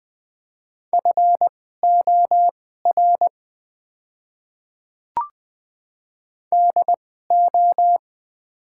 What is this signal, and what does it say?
Morse code, heard as a keyed tone, about 20 wpm, spaced as for 15 wpm, 700 Hz. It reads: FOR DO
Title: 2x Character Spacing - Top 200 Words - 20wpm